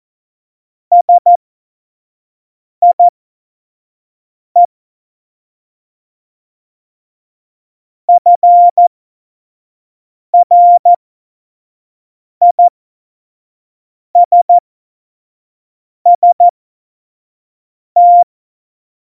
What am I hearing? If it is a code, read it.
Morse code, heard as a keyed tone, about 14 wpm, spaced as for 5 wpm, 700 Hz. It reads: SIE FRISST
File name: Morsekurs / Witze / Witz 25. Dezember 2018, 25 F70